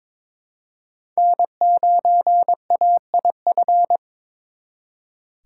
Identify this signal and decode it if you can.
Morse code, heard as a keyed tone, about 26 wpm, 700 Hz. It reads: N9AIF
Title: Speed Racing - USA 1x3 Call Signs 22wpm